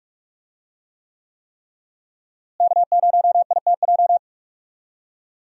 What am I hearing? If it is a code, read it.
Morse code, heard as a keyed tone, about 47 wpm, 700 Hz. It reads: K0ITJ
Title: USA Call Signs - 5 Characters - Repeated 3x - 45wpm